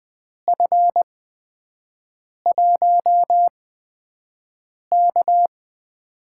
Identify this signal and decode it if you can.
Morse code, heard as a keyed tone, about 20 wpm, spaced as for 5 wpm, 700 Hz. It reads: F1K
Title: USA Call Signs - 4 Characters - 8x Character Spacing - 20wpm